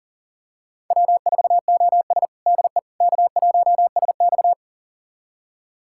Morse code, heard as a keyed tone, about 40 wpm, 700 Hz. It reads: W4OS DE K1SX
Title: USA Callsign DE Callsign - 4 Characters - 40wpm